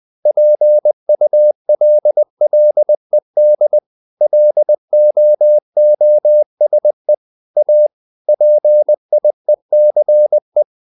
Morse code, heard as a keyed tone, about 20 wpm, 600 Hz. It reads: PULLED LOOSE A PIECE
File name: the old man and the sea part 2 20wpm